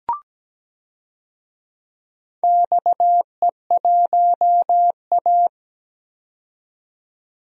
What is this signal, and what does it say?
Morse code, heard as a keyed tone, about 17 wpm, 700 Hz. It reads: XE1A
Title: DXCC-100 Call Signs - 4 Characters - 17wpm